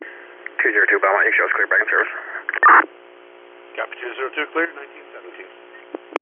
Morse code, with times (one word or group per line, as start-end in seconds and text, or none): none